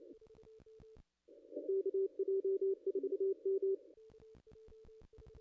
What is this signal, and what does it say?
Morse code, heard as a keyed tone, about 28 wpm, 380 Hz. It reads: KJ4M
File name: contests / ss-cw-k5tr-op-n5rz -0200